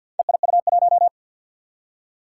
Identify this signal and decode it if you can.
Morse code, heard as a keyed tone, about 54 wpm, 700 Hz. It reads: EIU1
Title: Four Challenging Letter-Number Combinations 50wpm